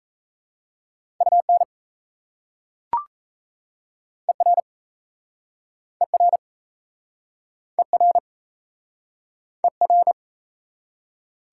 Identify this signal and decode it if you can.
Morse code, heard as a keyed tone, about 38 wpm, 700 Hz. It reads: UN ER ER ER ER